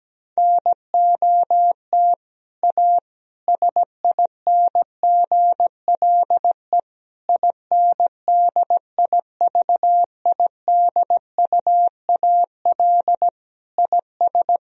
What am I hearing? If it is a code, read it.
Morse code, heard as a keyed tone, about 17 wpm, 700 Hz. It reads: NOT A SINGLE INDIVIDUAL IS
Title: Sentences from Top 1000 Words - Review Part 4 - 17wpm